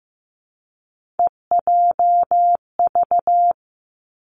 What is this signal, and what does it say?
Morse code, heard as a keyed tone, about 15 wpm, 700 Hz. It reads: EJV
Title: Three Challenging Letter-Number Combinations 15wpm